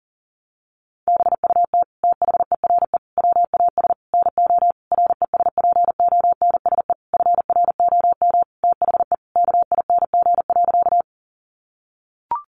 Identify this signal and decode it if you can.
Morse code, heard as a keyed tone, about 40 wpm, 700 Hz. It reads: BUT THERE WAS NO RESPONSE FROM THE KING.
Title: Sentences from Top 1400 Words 40wpm